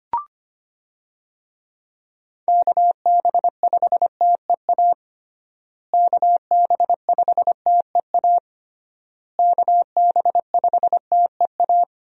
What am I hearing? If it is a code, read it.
Morse code, heard as a keyed tone, about 25 wpm, 700 Hz. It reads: KB5TEA KB5TEA KB5TEA
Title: USA Call Signs - 6 Characters - Repeated 3x - 25wpm